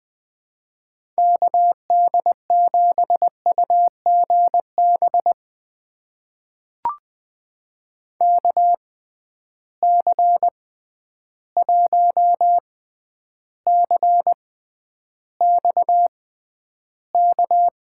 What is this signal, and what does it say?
Morse code, heard as a keyed tone, about 20 wpm, spaced as for 10 wpm, 700 Hz. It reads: KD7UGB KC1CXK
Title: USA Call Signs - 6 Characters - 6x Character Spacing - 20wpm